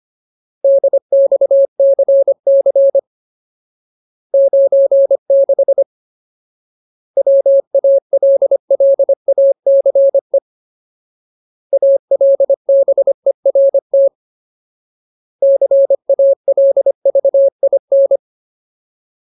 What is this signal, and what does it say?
Morse code, heard as a keyed tone, about 25 wpm, 550 Hz. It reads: DXCC 96 WALLACE ALBERT CALVIN
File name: Ultimate Word List / 25WPM Ultimate